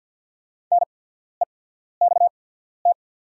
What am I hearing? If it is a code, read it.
Morse code, heard as a keyed tone, about 48 wpm, spaced as for 14 wpm, 700 Hz. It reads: NEXT
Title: 8x Character Spacing - Top 500 Words - 50wpm